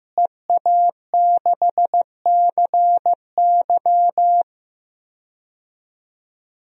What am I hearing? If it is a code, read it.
Morse code, heard as a keyed tone, about 15 wpm, 700 Hz. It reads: EA6CY
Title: DXCC-100 Call Signs - 5 Characters - Repeated 2x - 15wpm